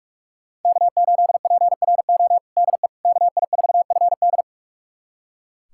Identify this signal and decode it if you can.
Morse code, heard as a keyed tone, about 41 wpm, 700 Hz. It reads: K8PRO DE KI4FD
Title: USA Callsign DE Callsign - 5 Characters - 45wpm